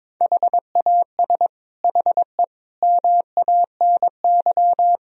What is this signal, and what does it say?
Morse code, heard as a keyed tone, about 22 wpm, 700 Hz. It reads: HAS HE MANY
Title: Sentences from Top 1000 Words - Review Part 1 - 22wpm